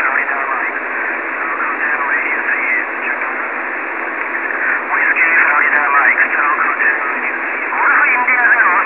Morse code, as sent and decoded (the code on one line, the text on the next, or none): none